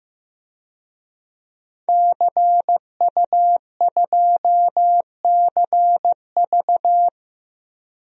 Morse code, heard as a keyed tone, about 15 wpm, 700 Hz. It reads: CU2CV
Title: DXCC-100 Call Signs - 5 Characters - Repeated 3x - 15wpm